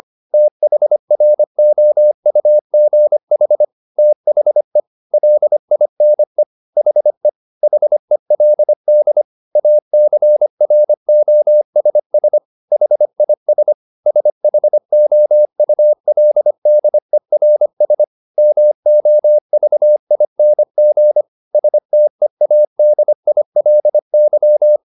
Morse code, heard as a keyed tone, about 25 wpm, 600 Hz. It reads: THROUGH THE LINE HE HELD ACROSS HIS SHOULDERS MOVING STEADILY
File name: the old man and the sea part 1 25wpm